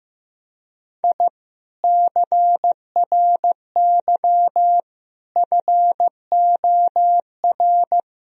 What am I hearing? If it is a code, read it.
Morse code, heard as a keyed tone, about 16 wpm, 700 Hz. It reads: I CRY FOR
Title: Sentences from Top 1000 Words - Review Part 2 - 15wpm